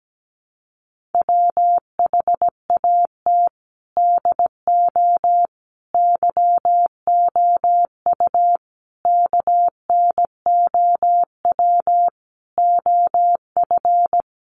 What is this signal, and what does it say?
Morse code, heard as a keyed tone, about 17 wpm, 700 Hz. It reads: WHAT DO YOU KNOW OF
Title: Sentences from Top 100 Words 17wpm